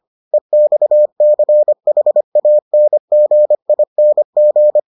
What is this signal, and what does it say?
Morse code, heard as a keyed tone, about 25 wpm, 600 Hz. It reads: EXCHANGING